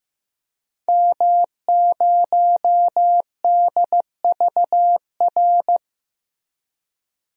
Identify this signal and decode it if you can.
Morse code, heard as a keyed tone, about 15 wpm, 700 Hz. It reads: M0DVR
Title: DXCC-100 Call Signs - 5 Characters - Repeated 2x - 15wpm